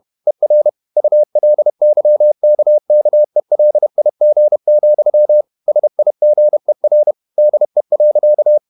Morse code, heard as a keyed tone, about 31 wpm, 600 Hz. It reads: ER ULYKKELIG, SIGER DE.
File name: Ferie QRQ 31wpm